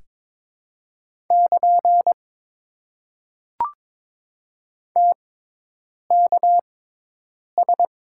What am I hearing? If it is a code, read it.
Morse code, heard as a keyed tone, about 22 wpm, spaced as for 8 wpm, 700 Hz. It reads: <KN> TKS
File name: Top 90 QSO Elements - 6x Character Spacing - 22wpm